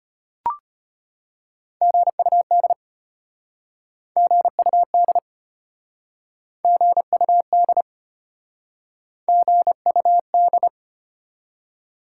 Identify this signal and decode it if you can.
Morse code, heard as a keyed tone, about 34 wpm, 690 Hz. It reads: GUD GUD GUD GUD